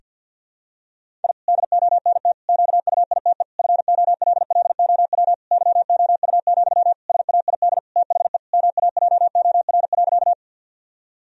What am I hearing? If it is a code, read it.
Morse code, heard as a keyed tone, about 49 wpm, 700 Hz. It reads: I DONT QUITE FOLLOW YOU, SAID THE MAJOR.
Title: Sentences from Top 1000 Words - Review Part 4 - 50wpm